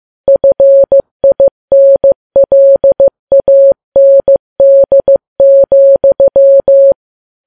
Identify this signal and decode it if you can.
Morse code, heard as a keyed tone, about 15 wpm, 550 Hz. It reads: FINLAND,
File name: Files Web tls / Morse on9fv / morse15kps